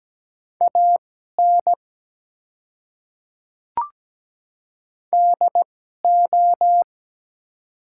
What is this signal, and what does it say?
Morse code, heard as a keyed tone, about 17 wpm, spaced as for 12 wpm, 700 Hz. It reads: AN DO
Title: 2x Character Spacing - Top 100 Words - 17wpm